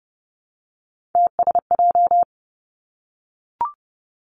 Morse code, heard as a keyed tone, about 30 wpm, 700 Hz. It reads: TSJ
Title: Three Challenging Letter-Number Combinations 30wpm